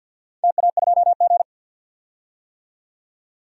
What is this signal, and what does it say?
Morse code, heard as a keyed tone, about 50 wpm, 700 Hz. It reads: TA2G
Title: DXCC-100 Call Signs - 4 Characters - 50wpm